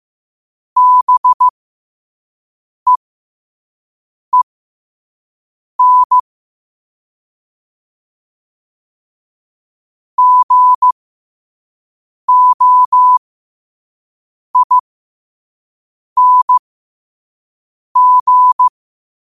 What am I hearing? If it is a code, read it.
Morse code, heard as a keyed tone, about 15 wpm, spaced as for 5 wpm, 1000 Hz. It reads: BEEN GOING
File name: qotdfiles / 5wpm